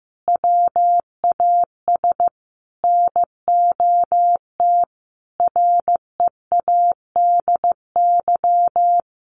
Morse code, heard as a keyed tone, about 15 wpm, 700 Hz. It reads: WAS NOT READY
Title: Sentences from Top 1500 Words 15wpm